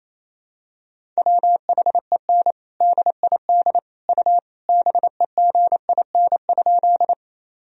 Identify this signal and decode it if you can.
Morse code, heard as a keyed tone, about 28 wpm, 700 Hz. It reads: WHEN DID U BEGIN?
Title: Sentences from Top 300 Words and Abbreviations 28wpm